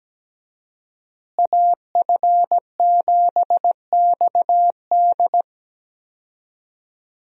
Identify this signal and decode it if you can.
Morse code, heard as a keyed tone, about 17 wpm, 700 Hz. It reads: AF7XD
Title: USA Call Signs - 5 Characters - Repeated 2x - 17wpm